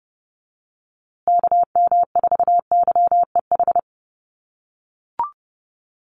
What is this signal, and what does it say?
Morse code, heard as a keyed tone, about 30 wpm, 700 Hz. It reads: KM4YEH
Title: USA Call Signs Repeated 2x 30wpm